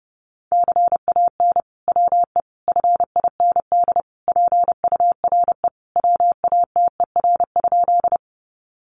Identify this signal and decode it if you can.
Morse code, heard as a keyed tone, about 30 wpm, 700 Hz. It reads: CAN WE FIND PURE WATER?
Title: Sentences from Top 1500 Words 30wpm